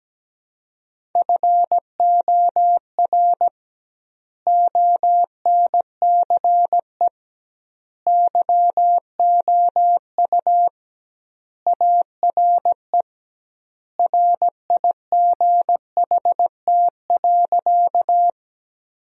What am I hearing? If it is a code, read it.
Morse code, heard as a keyed tone, about 17 wpm, 700 Hz. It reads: FOR ONCE YOU ARE RIGHT.
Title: 2x Word-Spacing - Sentences from Top 300 Words - 17wpm